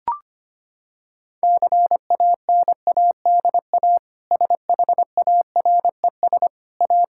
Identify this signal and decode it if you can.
Morse code, heard as a keyed tone, about 25 wpm, 700 Hz. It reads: CANADA SHARES A
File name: practice / 25wpm / Sentences with 4-Word Phrases - 25wpm